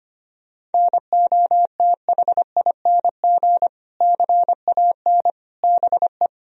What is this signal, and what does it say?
Morse code, heard as a keyed tone, about 25 wpm, 700 Hz. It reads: NOTHING CAN BE
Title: Sentences from Top 1000 Words - Review Part 2 - 25wpm